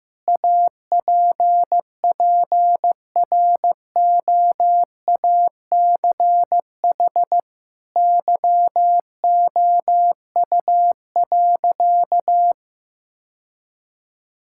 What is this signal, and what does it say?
Morse code, heard as a keyed tone, about 15 wpm, 700 Hz. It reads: APPROACH YOU.
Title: Sentences from Top 1000 Words - Review Part 6 - 15wpm